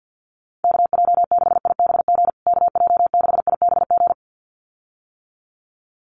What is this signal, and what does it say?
Morse code, heard as a keyed tone, about 50 wpm, 700 Hz. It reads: KJ6IBG KJ6IBG
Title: USA Call Signs Repeated 2x 50wpm